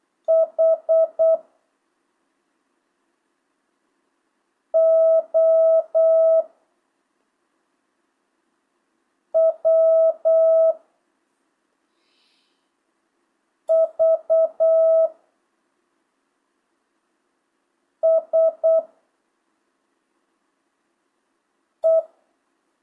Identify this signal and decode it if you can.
Morse code, heard as a keyed tone, about 8 wpm, 650 Hz. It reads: H O W V S E
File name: jh8cbh / 2025cw